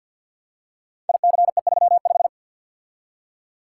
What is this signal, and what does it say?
Morse code, heard as a keyed tone, about 53 wpm, 700 Hz. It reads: ICE35
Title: Code Groups - 5 Letter:Numbers - 50wpm